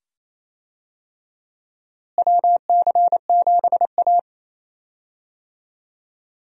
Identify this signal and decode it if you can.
Morse code, heard as a keyed tone, about 28 wpm, 700 Hz. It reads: WC7A
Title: USA Call Signs - 4 Characters - Repeated 2x - 28wpm